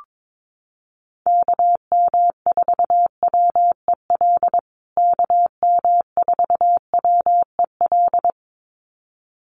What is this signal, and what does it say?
Morse code, heard as a keyed tone, about 22 wpm, 700 Hz. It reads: KM4WEL KM4WEL
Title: USA Call Signs Repeated 2x 22wpm